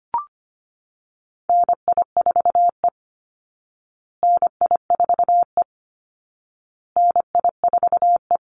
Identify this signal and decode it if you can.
Morse code, heard as a keyed tone, about 25 wpm, 700 Hz. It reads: NI4E NI4E NI4E
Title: Call Signs Repeated 2x - 25wpm